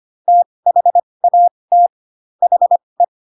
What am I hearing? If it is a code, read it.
Morse code, heard as a keyed tone, about 25 wpm, spaced as for 20 wpm, 700 Hz. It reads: THAT HE
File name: Pinocchio / Pinocchio ch7 15 25 ik1hge